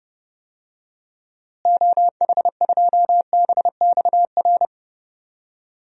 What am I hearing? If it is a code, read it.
Morse code, heard as a keyed tone, about 32 wpm, 700 Hz. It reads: OH2BXR
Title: DXCC-100 Call Signs - 6 Characters - Repeated 2x - 30wpm